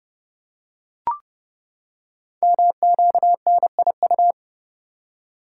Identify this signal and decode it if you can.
Morse code, heard as a keyed tone, about 30 wpm, 700 Hz. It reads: MQNIU
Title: Code Groups - 5 Letters - 30wpm